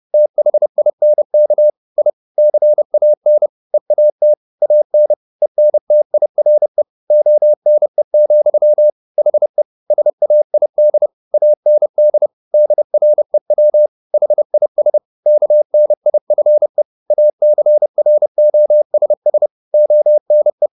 Morse code, heard as a keyed tone, about 30 wpm, 600 Hz. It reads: THINK I CAN EAT AN ENTIRE ONE, HE SAID AND DREW HIS KNIFE ACROSS ONE
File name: the old man and the sea / the old man and the sea 1 30wpm